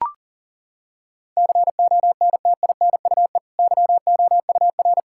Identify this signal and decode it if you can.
Morse code, heard as a keyed tone, about 40 wpm, 700 Hz. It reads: CONTINUE YOUR
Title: Sentences from Top 1000 Words - Review Part 6 - 40wpm